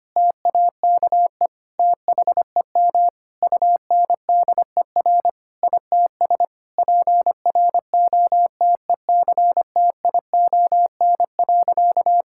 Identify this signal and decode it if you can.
Morse code, heard as a keyed tone, about 25 wpm, 700 Hz. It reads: TAKE THEM UNDER ITS PROTECTION.
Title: Sentences from Top 1600 Words 25wpm